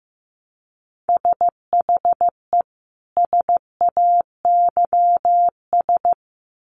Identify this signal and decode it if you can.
Morse code, heard as a keyed tone, about 15 wpm, 700 Hz. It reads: SHE SAYS
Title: Sets of 2 Words 15wpm